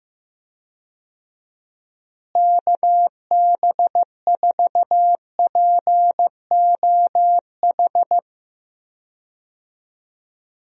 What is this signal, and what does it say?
Morse code, heard as a keyed tone, about 16 wpm, 700 Hz. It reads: KB4POH
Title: USA Call Signs - 6 Characters - Repeated 3x - 15wpm